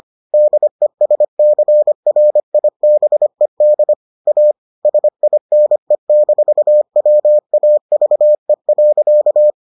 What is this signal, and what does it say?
Morse code, heard as a keyed tone, about 25 wpm, 600 Hz. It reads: DESCRIBED A SINE-WAVE.